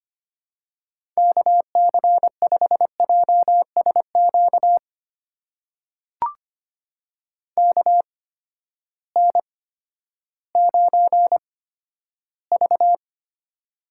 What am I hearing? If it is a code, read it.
Morse code, heard as a keyed tone, about 25 wpm, spaced as for 12 wpm, 700 Hz. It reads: KC5JSQ KN9V
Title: USA Call Signs - 8x Character Spacing - 25wpm